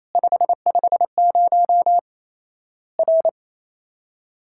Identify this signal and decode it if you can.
Morse code, heard as a keyed tone, about 27 wpm, 700 Hz. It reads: 550 R
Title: Worked All Europe DX Contest - QTC Exchange - 28wpm